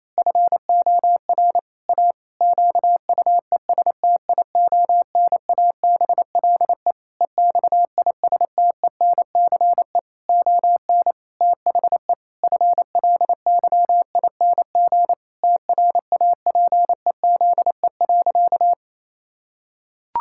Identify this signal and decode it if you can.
Morse code, heard as a keyed tone, about 28 wpm, 700 Hz. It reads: FOR A QUESTIONABLE EXISTENCE ON THE FLYING TRAPEZE.
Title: Pangram Challenge v2 - 28wpm